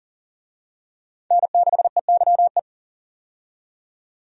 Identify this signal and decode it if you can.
Morse code, heard as a keyed tone, about 40 wpm, 700 Hz. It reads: N6EYE
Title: USA Call Signs - 5 Characters - 40wpm